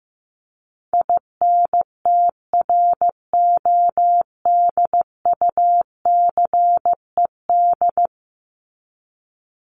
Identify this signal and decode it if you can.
Morse code, heard as a keyed tone, about 15 wpm, 700 Hz. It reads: INTRODUCED
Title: ED Suffix - Top 100 Words with Base Words 15wpm